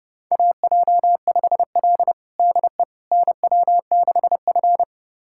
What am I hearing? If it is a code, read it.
Morse code, heard as a keyed tone, about 30 wpm, 700 Hz. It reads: AJ5L DE NW6F
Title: USA Callsign DE Callsign - 4 Characters - 30wpm